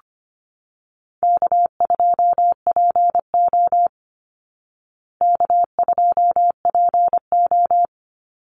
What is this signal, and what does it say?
Morse code, heard as a keyed tone, about 25 wpm, 700 Hz. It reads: K2PO K2PO
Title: Call Signs Repeated 3x - 25wpm